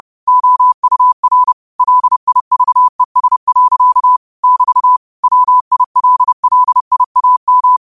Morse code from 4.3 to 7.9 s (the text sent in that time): <BT> WILLIAM